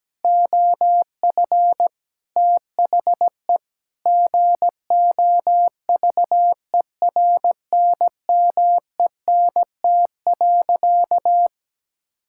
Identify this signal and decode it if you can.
Morse code, heard as a keyed tone, about 17 wpm, 700 Hz. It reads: OF THE GOVERNMENT.
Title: Sentences from Top 1600 Words 17wpm